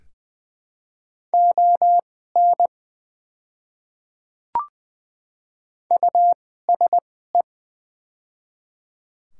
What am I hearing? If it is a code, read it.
Morse code, heard as a keyed tone, about 20 wpm, spaced as for 15 wpm, 700 Hz. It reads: ON USE
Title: 2x Character Spacing - Top 300 Words - 20wpm